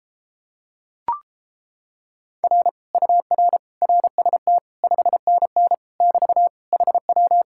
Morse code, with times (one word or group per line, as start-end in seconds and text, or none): 2.4-2.7 R
2.9-3.6 UR
3.8-4.6 RST
4.8-5.7 5NN
6.0-6.5 <BT>
6.7-7.4 HW